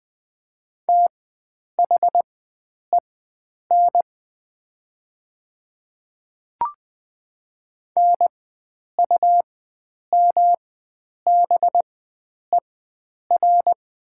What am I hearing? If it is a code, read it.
Morse code, heard as a keyed tone, about 20 wpm, spaced as for 9 wpm, 700 Hz. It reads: THEN NUMBER
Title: 4x Character Spacing - Top 100 Words - 20wpm